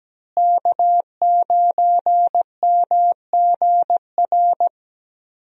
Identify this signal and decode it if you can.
Morse code, heard as a keyed tone, about 17 wpm, 700 Hz. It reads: K9MGR
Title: USA Call Signs - 5 Characters - Repeated 3x - 17wpm